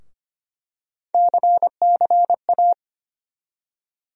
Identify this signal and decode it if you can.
Morse code, heard as a keyed tone, about 25 wpm, 700 Hz. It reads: CCA
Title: USA Call Sign Postfixes - 3 Letters 25wpm